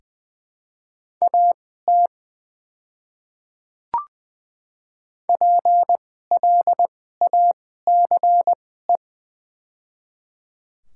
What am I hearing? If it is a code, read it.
Morse code, heard as a keyed tone, about 21 wpm, spaced as for 15 wpm, 700 Hz. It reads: AT PLACE
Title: 2x Character Spacing - Top 200 Words - 20wpm